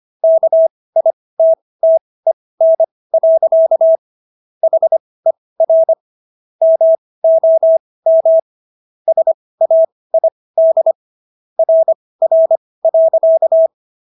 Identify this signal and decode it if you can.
Morse code, heard as a keyed tone, about 25 wpm, spaced as for 18 wpm, 650 Hz. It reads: KITTEN. HER MOM SAID RR.